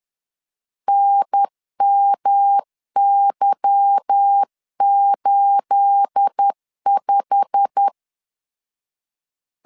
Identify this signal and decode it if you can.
Morse code, heard as a keyed tone, about 10 wpm, 800 Hz. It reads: NMY85
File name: LEZIONI / 9 28 10wpm 2842